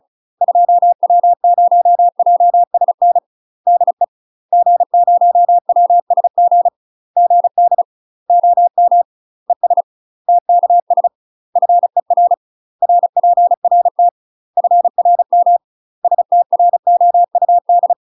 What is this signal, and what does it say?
Morse code, heard as a keyed tone, about 35 wpm, 700 Hz. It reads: 2W0JSN DE G0WSG GD OM ES TKS FER RPRT FRM STROUD
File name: UK MASTER QSOs TRAINING / 35WPM / UK MASTER QSOs 0010